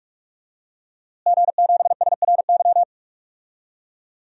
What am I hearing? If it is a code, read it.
Morse code, heard as a keyed tone, about 42 wpm, 700 Hz. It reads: G7SRY
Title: DXCC-100 Call Signs - 5 Characters - Repeated 2x - 45wpm